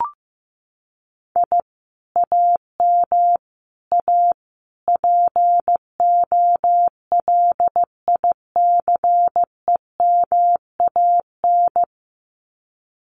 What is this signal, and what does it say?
Morse code, heard as a keyed tone, about 15 wpm, 700 Hz. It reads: I AM A POLICEMAN
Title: Common QSO Phrases Encore 15wpm